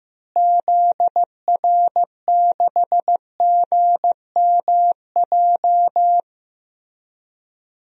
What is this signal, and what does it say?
Morse code, heard as a keyed tone, about 15 wpm, 700 Hz. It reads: ZR6GMJ
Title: DXCC-100 Call Signs - 6 Characters - Repeated 3x - 15wpm